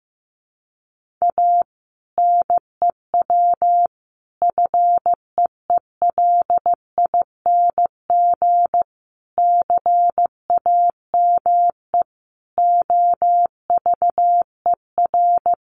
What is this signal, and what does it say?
Morse code, heard as a keyed tone, about 15 wpm, 700 Hz. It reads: A NEW FEELING CAME OVER
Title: Sentences from Top 1200 Words 15wpm